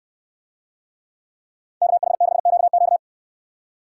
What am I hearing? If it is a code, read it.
Morse code, heard as a keyed tone, about 69 wpm, 700 Hz. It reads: KH6QQ